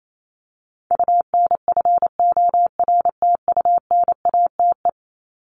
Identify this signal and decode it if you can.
Morse code, heard as a keyed tone, about 28 wpm, 700 Hz. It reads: UNFORTUNATE
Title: LY Suffix - Top 100 Words with Base Words 28wpm